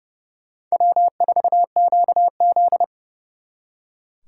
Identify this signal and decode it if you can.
Morse code, heard as a keyed tone, about 38 wpm, 700 Hz. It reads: W4QZ